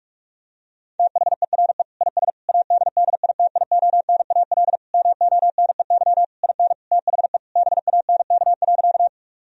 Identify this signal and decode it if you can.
Morse code, heard as a keyed tone, about 41 wpm, 700 Hz. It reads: THERE IS ADDITIONAL MONEY IN THE BANK.
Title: Sentences from Top 1000 Words - Review Part 4 - 45wpm